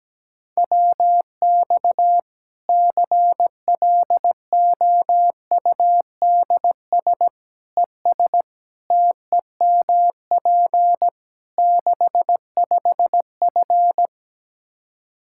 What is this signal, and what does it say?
Morse code, heard as a keyed tone, about 17 wpm, 700 Hz. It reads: WX CLOUDS ES TEMP 65F
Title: Practice QSO - Exchange 2 - Info - 17wpm